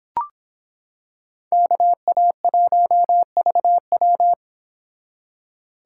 Morse code, heard as a keyed tone, about 26 wpm, 700 Hz. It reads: KA1VW